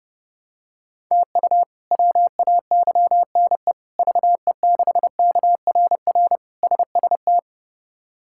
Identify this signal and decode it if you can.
Morse code, heard as a keyed tone, about 32 wpm, 700 Hz. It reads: TU WAYNE VE6KRR SST
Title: K1USN SST QSO Practice - 30wpm